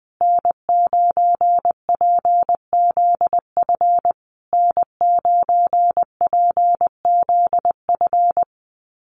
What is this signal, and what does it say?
Morse code, heard as a keyed tone, about 20 wpm, 700 Hz. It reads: N9PZF N9PZF
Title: USA Call Signs Repeated 3x 20wpm